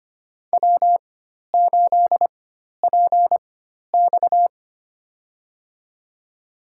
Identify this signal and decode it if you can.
Morse code, heard as a keyed tone, about 25 wpm, spaced as for 12 wpm, 700 Hz. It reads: W8PX
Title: USA Call Signs - 4 Characters - 4x Character Spacing - 25wpm